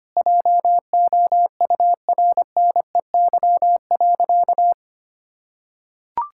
Morse code, heard as a keyed tone, about 25 wpm, 700 Hz. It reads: JOURNEY.
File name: practice / 25wpm / Sentences with 9-Word Phrases - 25wpm